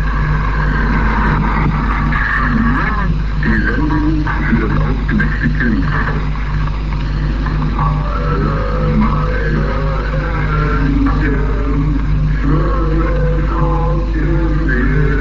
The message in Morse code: none